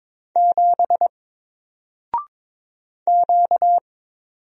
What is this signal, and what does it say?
Morse code, heard as a keyed tone, about 22 wpm, 700 Hz. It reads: 7 Q
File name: Single-Letter-Number-22wpm